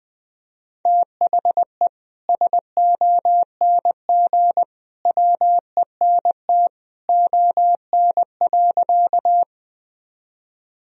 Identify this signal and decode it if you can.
Morse code, heard as a keyed tone, about 21 wpm, 700 Hz. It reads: THE SONG WENT ON.
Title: Sentences from Top 1000 Words - Review Part 2 - 20wpm